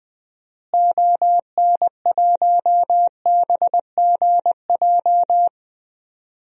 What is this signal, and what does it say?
Morse code, heard as a keyed tone, about 20 wpm, 700 Hz. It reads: ON1BGJ
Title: DXCC-100 Call Signs - 6 Characters - Repeated 2x - 20wpm